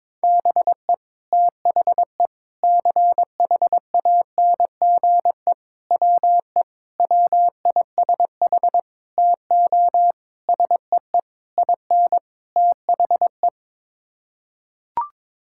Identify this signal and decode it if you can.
Morse code, heard as a keyed tone, about 22 wpm, 700 Hz. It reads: BE THE CHANGE WE WISH TO SEE IN THE
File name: Sets of 9 Words - 22wpm